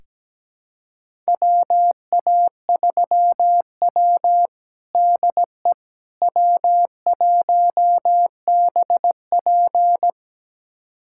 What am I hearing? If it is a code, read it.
Morse code, heard as a keyed tone, about 17 wpm, 700 Hz. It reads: WA3W DE W1BP
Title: USA Callsign DE Callsign - 4 Characters - 17wpm